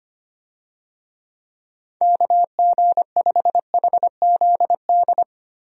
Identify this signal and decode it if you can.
Morse code, heard as a keyed tone, about 25 wpm, 700 Hz. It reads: KG5HZD
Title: USA Call Signs - 6 Characters - 25wpm